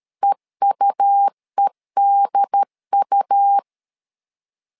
Morse code, heard as a keyed tone, about 13 wpm, 800 Hz. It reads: EUEDU
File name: LEZIONI / 3 MWRO / 3 MWRO 12wpm 1730